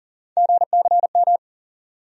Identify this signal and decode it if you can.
Morse code, heard as a keyed tone, about 40 wpm, 700 Hz. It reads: GCM
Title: USA Call Sign Postfixes - 3 Letters 40wpm